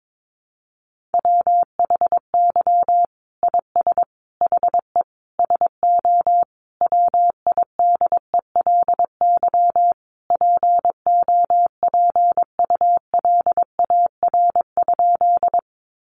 Morse code, heard as a keyed tone, about 22 wpm, 700 Hz. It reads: WHY IS HE SO WIDELY POPULAR?
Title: Sentences from Top 1200 Words 22wpm